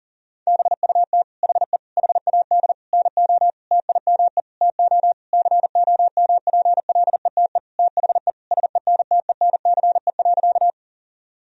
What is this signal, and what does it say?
Morse code, heard as a keyed tone, about 40 wpm, 700 Hz. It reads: BUT HE HAD NO TIME TO COMPLETE THE SENTENCE.
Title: Sentences from Top 1000 Words - Review Part 1 - 40wpm